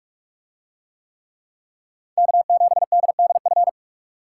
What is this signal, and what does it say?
Morse code, heard as a keyed tone, about 40 wpm, 700 Hz. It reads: K7DDF